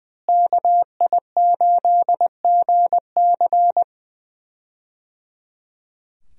Morse code, heard as a keyed tone, about 21 wpm, 700 Hz. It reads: KI8GC
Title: USA Call Signs - 5 Characters - Repeated 3x - 20wpm